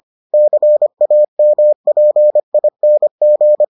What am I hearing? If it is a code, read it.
Morse code, heard as a keyed tone, about 25 wpm, 600 Hz. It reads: CAMPING